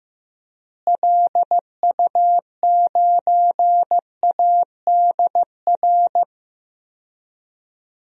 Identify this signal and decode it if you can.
Morse code, heard as a keyed tone, about 15 wpm, 700 Hz. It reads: LU9ADR
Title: DXCC-100 Call Signs - 6 Characters - Repeated 2x - 15wpm